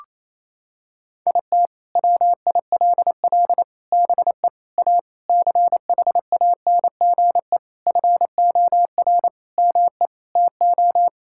Morse code, heard as a keyed tone, about 28 wpm, 700 Hz. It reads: IT WILL BE A CHANGE FOR ME TO
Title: Sentences from Top 1000 Words - Review Part 6 - 28wpm